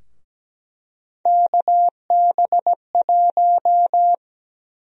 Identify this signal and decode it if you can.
Morse code, heard as a keyed tone, about 17 wpm, 700 Hz. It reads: KB1
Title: USA Call Sign Prefixes - Repeated 2x - 17wpm